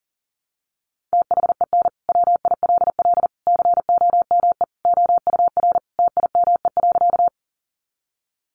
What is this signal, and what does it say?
Morse code, heard as a keyed tone, about 40 wpm, 700 Hz. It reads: THEN WILL COME OUR TIME.
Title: Sentences from Top 200 Words 40wpm